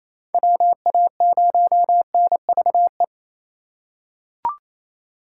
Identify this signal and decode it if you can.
Morse code, heard as a keyed tone, about 28 wpm, 700 Hz. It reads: WA0NVE
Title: USA Call Signs - 6 Characters - 2x Character Spacing - 28wpm